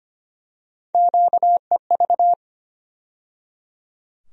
Morse code, heard as a keyed tone, about 25 wpm, 700 Hz. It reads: QEV
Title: USA Call Sign Postfixes - 3 Letters 25wpm